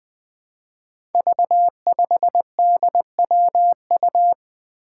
Morse code, heard as a keyed tone, about 21 wpm, 700 Hz. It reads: V5DWU
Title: Five Challenging Letter-Number Combinations 20wpm